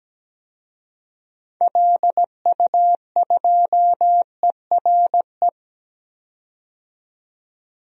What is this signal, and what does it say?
Morse code, heard as a keyed tone, about 17 wpm, 700 Hz. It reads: LU2ERE
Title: DXCC-100 Call Signs - 6 Characters - 17wpm